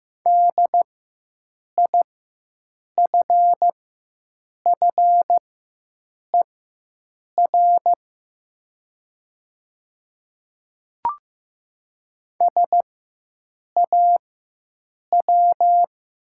Morse code, heard as a keyed tone, about 15 wpm, spaced as for 7 wpm, 700 Hz. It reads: DIFFER SAW
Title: 4x Character Spacing - Top 500 Words - 15wpm